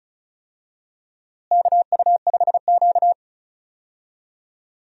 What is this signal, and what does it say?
Morse code, heard as a keyed tone, about 36 wpm, 700 Hz. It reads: KU5Q
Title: USA Call Signs - 4 Characters - Repeated 3x - 35wpm